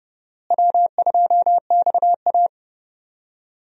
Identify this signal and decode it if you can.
Morse code, heard as a keyed tone, about 31 wpm, 700 Hz. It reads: W2XA
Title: USA Call Signs - 4 Characters - Repeated 2x - 30wpm